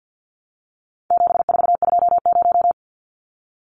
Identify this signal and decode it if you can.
Morse code, heard as a keyed tone, about 50 wpm, 700 Hz. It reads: 7420
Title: Numbers - 4-Digit 50wpm